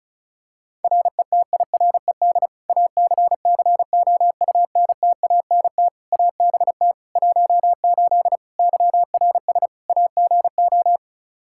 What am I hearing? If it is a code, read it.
Morse code, heard as a keyed tone, about 34 wpm, 700 Hz. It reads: RETIRED ACCOUNTANT ABT 18 YRS AGO
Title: K1USN SST QSO Practice 35wpm